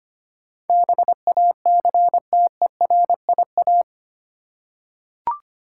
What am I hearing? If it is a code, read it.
Morse code, heard as a keyed tone, about 25 wpm, 700 Hz. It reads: BACTERIA
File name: Eight Letter Words - 25wpm